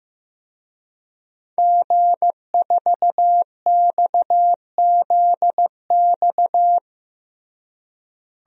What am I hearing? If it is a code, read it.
Morse code, heard as a keyed tone, about 16 wpm, 700 Hz. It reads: G4XZX
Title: DXCC-100 Call Signs - 5 Characters - Repeated 3x - 15wpm